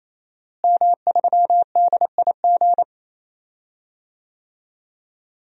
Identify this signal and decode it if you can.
Morse code, heard as a keyed tone, about 28 wpm, 700 Hz. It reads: M3DIG
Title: DXCC-100 Call Signs - 5 Characters - Repeated 3x - 28wpm